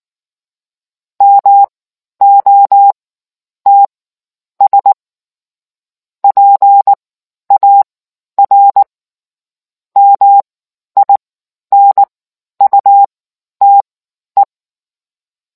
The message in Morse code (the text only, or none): MOTS PAR MINUTE